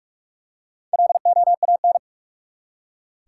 Morse code, heard as a keyed tone, about 46 wpm, 700 Hz. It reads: LOAN
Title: Four Letter Words - 45wpm